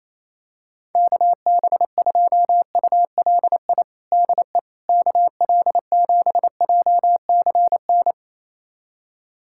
Morse code, heard as a keyed tone, about 28 wpm, 700 Hz. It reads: KB2ULI DE KL7JCN
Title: USA Callsign DE Callsign 28wpm